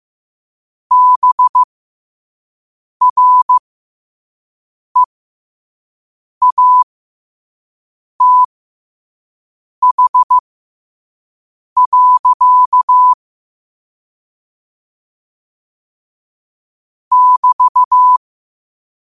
Morse code, heard as a keyed tone, about 15 wpm, spaced as for 5 wpm, 1000 Hz. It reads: BREATH. <BT>
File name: qotdfiles / 5wpm